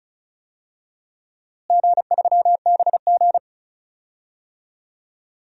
Morse code, heard as a keyed tone, about 36 wpm, 700 Hz. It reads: G3BG
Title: DXCC-100 Call Signs - 4 Characters - Repeated 2x - 35wpm